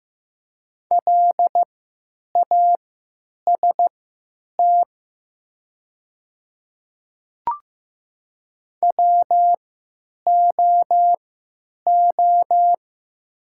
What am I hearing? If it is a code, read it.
Morse code, heard as a keyed tone, about 15 wpm, spaced as for 9 wpm, 700 Hz. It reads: LAST WOO
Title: 3x Character Spacing - Top 300 Words - 15wpm